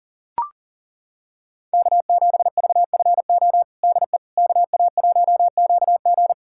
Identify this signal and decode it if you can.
Morse code, heard as a keyed tone, about 40 wpm, 700 Hz. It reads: K7VFO DE KA1QG
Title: USA Callsign DE Callsign - 5 Characters - 40wpm